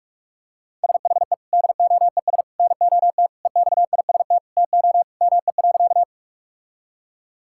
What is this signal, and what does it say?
Morse code, heard as a keyed tone, about 41 wpm, 690 Hz. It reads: SHE DOES NOT EXIST TO ME.
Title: Sentences from Top 1000 Words - Review Part 5 - 45wpm